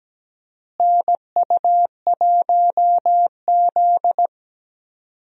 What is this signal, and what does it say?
Morse code, heard as a keyed tone, about 17 wpm, 700 Hz. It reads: NU1Z
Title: USA Call Signs - 4 Characters - Repeated 2x - 17wpm